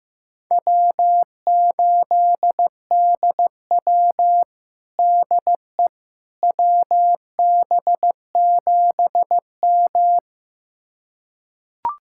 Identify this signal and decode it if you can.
Morse code, heard as a keyed tone, about 15 wpm, 700 Hz. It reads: W8DW DE WB7M
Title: USA Callsign DE Callsign - 4 Characters - 15wpm